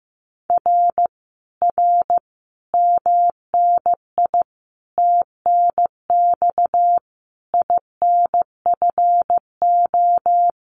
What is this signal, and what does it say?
Morse code, heard as a keyed tone, about 15 wpm, 700 Hz. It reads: R R MNI TNX INFO
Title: Common QSO Phrases Encore 15wpm